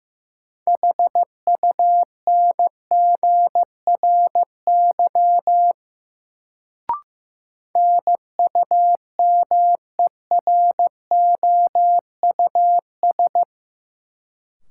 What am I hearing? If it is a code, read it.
Morse code, heard as a keyed tone, about 15 wpm, 700 Hz. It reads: HUNGRY NUMEROUS
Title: Sentences from Top 1400 Words - New Word Drill - 15wpm